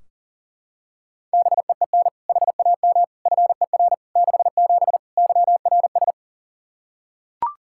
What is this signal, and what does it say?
Morse code, heard as a keyed tone, about 37 wpm, 700 Hz. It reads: BEEN HAM FER 67 YRS
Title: Practice QSO - Exchange 3 - More Info - 40wpm